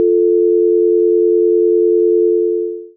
Dial tone (North America), 350 + 440 Hz.
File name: Office16 / LYNC abbrdialtone